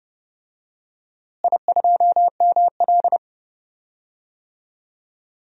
Morse code, heard as a keyed tone, about 30 wpm, 700 Hz. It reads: I2ML
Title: DXCC-100 Call Signs - 4 Characters - Repeated 2x - 30wpm